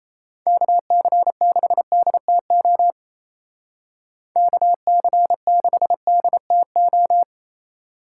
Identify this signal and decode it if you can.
Morse code, heard as a keyed tone, about 35 wpm, 700 Hz. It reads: KC6DTO KC6DTO